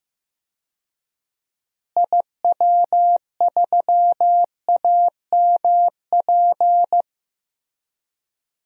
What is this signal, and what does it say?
Morse code, heard as a keyed tone, about 15 wpm, 700 Hz. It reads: IW3AMP
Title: DXCC-100 Call Signs - 6 Characters - Repeated 3x - 15wpm